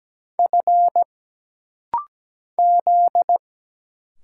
Morse code, heard as a keyed tone, about 17 wpm, 700 Hz. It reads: F Z
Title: Single Letters - Rapid-Fire 17wpm